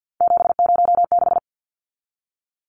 Morse code, heard as a keyed tone, about 50 wpm, 700 Hz. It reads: 706
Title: Numbers - 3-Digit 50wpm